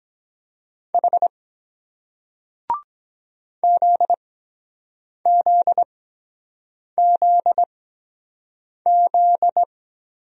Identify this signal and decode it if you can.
Morse code, heard as a keyed tone, about 23 wpm, spaced as for 7 wpm, 700 Hz. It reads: H ZZZZ